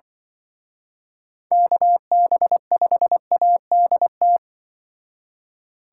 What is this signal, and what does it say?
Morse code, heard as a keyed tone, about 26 wpm, 700 Hz. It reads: KB5ADT